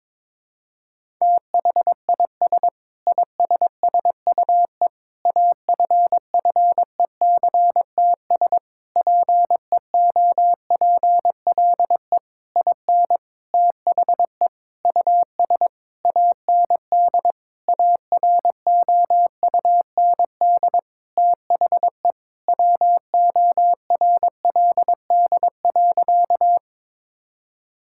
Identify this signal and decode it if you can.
Morse code, heard as a keyed tone, about 22 wpm, 700 Hz. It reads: THIS ISSUE AFFECTS PEOPLE IN THE US AND AROUND THE WORLD.
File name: Sentences with 7-Word Phrases - 22wpm